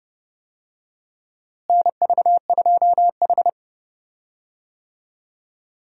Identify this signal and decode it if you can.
Morse code, heard as a keyed tone, about 32 wpm, 700 Hz. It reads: NV2H
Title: USA Call Signs - 4 Characters - Repeated 2x - 30wpm